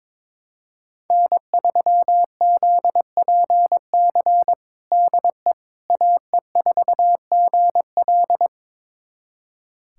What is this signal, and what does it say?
Morse code, heard as a keyed tone, about 22 wpm, 700 Hz. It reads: N3ZPC DE AE4GL
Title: USA Callsign DE Callsign - 5 Characters - 22wpm